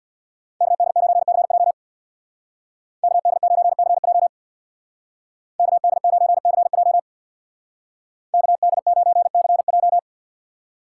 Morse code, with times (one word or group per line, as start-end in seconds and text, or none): 0.6-1.7 KD9CJ
3.0-4.3 KD9CJ
5.6-7.0 KD9CJ
8.3-10.0 KD9CJ